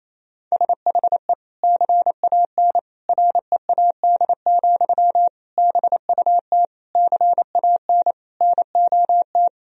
Morse code, heard as a keyed tone, about 27 wpm, 700 Hz. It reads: SHE CAN READ, BUT CAN NOT
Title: Sentences from Top 1000 Words - Review Part 1 - 28wpm